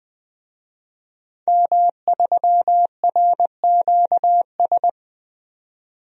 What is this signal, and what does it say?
Morse code, heard as a keyed tone, about 20 wpm, 700 Hz. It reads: M3RQS
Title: DXCC-100 Call Signs - 5 Characters - 20wpm